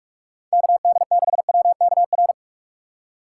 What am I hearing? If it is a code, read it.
Morse code, heard as a keyed tone, about 46 wpm, 700 Hz. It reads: KD6WKR